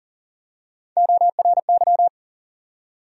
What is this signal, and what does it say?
Morse code, heard as a keyed tone, about 40 wpm, 700 Hz. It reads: ORY
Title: USA Call Sign Postfixes - 3 Letters 40wpm